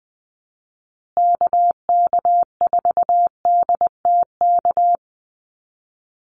Morse code, heard as a keyed tone, about 20 wpm, 700 Hz. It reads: KK4DTK
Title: USA Call Signs Repeated 2x 20wpm